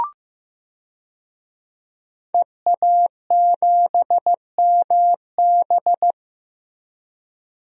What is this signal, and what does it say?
Morse code, heard as a keyed tone, about 16 wpm, 700 Hz. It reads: EA7MB
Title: DXCC-100 Call Signs - 5 Characters - Repeated 2x - 15wpm